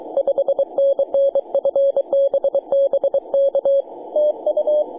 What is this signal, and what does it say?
Morse code, heard as a keyed tone, about 24 wpm, 590 Hz. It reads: 5NN FB BK TU
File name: posts / may-04-0625z-je2ehp-jd1-10-122mhz-cw